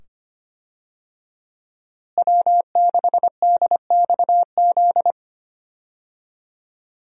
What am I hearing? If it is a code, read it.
Morse code, heard as a keyed tone, about 25 wpm, 700 Hz. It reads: W6DXZ
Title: USA Call Signs - 5 Characters - Repeated 2x - 25wpm